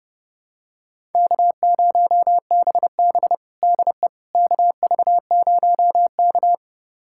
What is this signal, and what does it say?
Morse code, heard as a keyed tone, about 30 wpm, 700 Hz. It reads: K0BB DE KV0K
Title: USA Callsign DE Callsign - 4 Characters - 30wpm